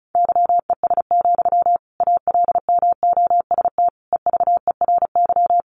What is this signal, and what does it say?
Morse code, heard as a keyed tone, about 34 wpm, 700 Hz. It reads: YES, ALMOST EVERY
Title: Sentences from Top 700 Words 35wpm